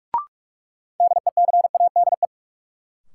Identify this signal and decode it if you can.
Morse code, heard as a keyed tone, about 46 wpm, 700 Hz. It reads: DECADE
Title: Sets of 5 Words - New Word Drill - 45wpm